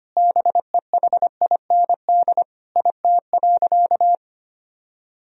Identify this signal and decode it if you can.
Morse code, heard as a keyed tone, about 25 wpm, 700 Hz. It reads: BEHIND IT.
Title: Sentences from Top 500 Words and Abbreviations 25wpm